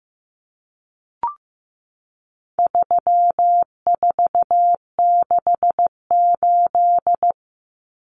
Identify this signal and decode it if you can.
Morse code, heard as a keyed tone, about 15 wpm, 700 Hz. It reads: 3468
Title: Numbers - 4-Digit 15wpm